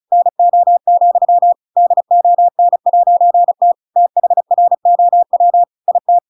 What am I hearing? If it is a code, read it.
Morse code, heard as a keyed tone, about 35 wpm, 700 Hz. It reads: NO, DON'T THROW IT
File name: Pinocchio / Pinocchio ch7 35 35 ik1hge